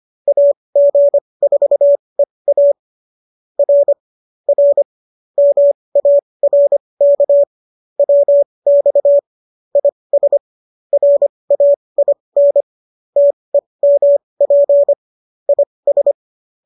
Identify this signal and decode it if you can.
Morse code, heard as a keyed tone, about 25 wpm, spaced as for 20 wpm, 570 Hz. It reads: AG4EA R R MARK WX IS RAIN TEMP IS